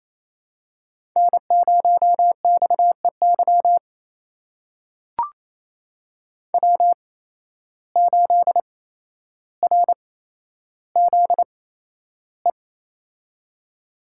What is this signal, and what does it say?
Morse code, heard as a keyed tone, about 28 wpm, spaced as for 12 wpm, 700 Hz. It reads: N0XEY W8RZE
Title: USA Call Signs - 5 Characters - 8x Character Spacing - 28wpm